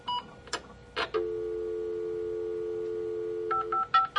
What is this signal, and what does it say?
Dial tone (North America), 350 + 440 Hz.
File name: voice / shenghuo